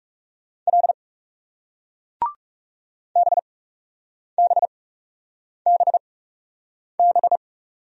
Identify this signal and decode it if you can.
Morse code, heard as a keyed tone, about 40 wpm, spaced as for 8 wpm, 700 Hz. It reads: L BBBB